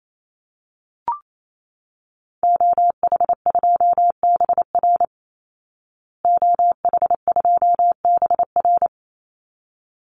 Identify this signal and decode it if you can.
Morse code, heard as a keyed tone, about 28 wpm, 700 Hz. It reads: OH2BR OH2BR
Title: Call Signs Repeated 3x - 28wpm